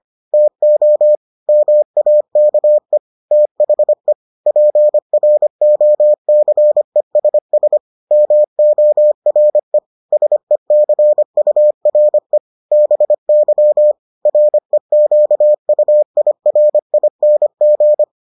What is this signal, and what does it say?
Morse code, heard as a keyed tone, about 25 wpm, 600 Hz. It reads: TO MAKE THE PROCESS MORE SECURE BY REQUIRING